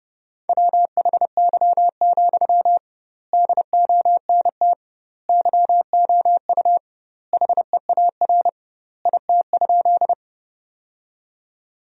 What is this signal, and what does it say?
Morse code, heard as a keyed tone, about 30 wpm, 700 Hz. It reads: WHY, DONT YOU HEAR IT?
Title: 2x Word-Spacing - Sentences from Top 300 Words - 30wpm